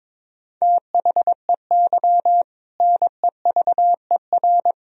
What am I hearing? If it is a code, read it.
Morse code, heard as a keyed tone, about 22 wpm, 700 Hz. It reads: THEY NEVER
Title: Sentences from Top 300 Words and Abbreviations 22wpm